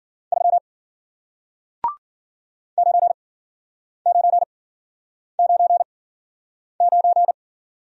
Morse code, heard as a keyed tone, about 54 wpm, spaced as for 9 wpm, 700 Hz. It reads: 3 9999